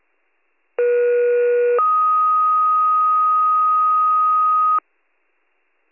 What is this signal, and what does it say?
Two-tone page, heard as a one-second tone, then a three-second tone, about 480 then 1250 Hz. It is heard from 0.8 to 4.8 s.